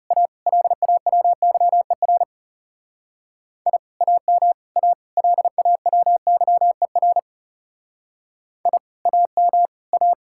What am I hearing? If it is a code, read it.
Morse code, heard as a keyed tone, about 38 wpm, 700 Hz. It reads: A LAWYER I AM A LAWYER I AM A